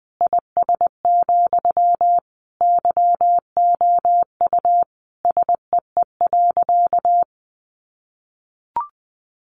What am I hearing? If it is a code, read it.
Morse code, heard as a keyed tone, about 20 wpm, 700 Hz. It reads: IS, YOU SEE.
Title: Sentences from Top 100 Words 20wpm